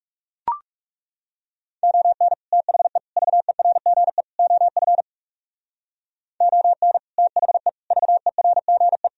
Morse code, heard as a keyed tone, about 37 wpm, 700 Hz. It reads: ON THE VERGE OF ON THE VERGE
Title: Speed-Racing - Sets of 4 Words 30wpm